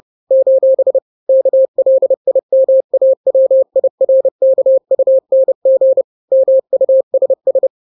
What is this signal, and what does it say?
Morse code, heard as a keyed tone, about 30 wpm, 530 Hz. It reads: : KLIMAWIRKUNG MUSS
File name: 29wpm